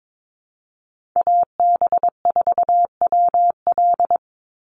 Morse code, heard as a keyed tone, about 22 wpm, 700 Hz. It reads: AB4WL
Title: NA SOTA Chaser Callsigns 22wpm